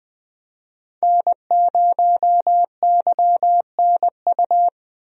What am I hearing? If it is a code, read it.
Morse code, heard as a keyed tone, about 20 wpm, 700 Hz. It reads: N0YNU
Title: USA Call Signs - 5 Characters - Repeated 2x - 20wpm